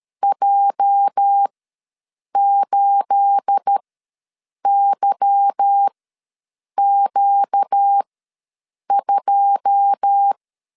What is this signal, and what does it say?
Morse code, heard as a keyed tone, about 13 wpm, spaced as for 7 wpm, 800 Hz. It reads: J8YQ2
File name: LEZIONI / 10 SIGNS / 10A SLASH ws 12wpm 3443